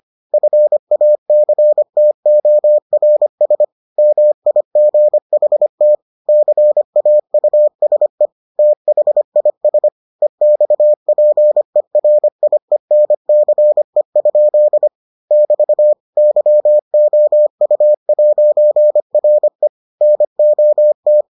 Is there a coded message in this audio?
Yes: Morse code from 0.3 to 21.2 s, heard as a keyed tone, about 25 wpm, 600 Hz. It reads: FACTORS MIGHT CAUSE THIS EXPERIENCE? <BT> YOU'RE NOT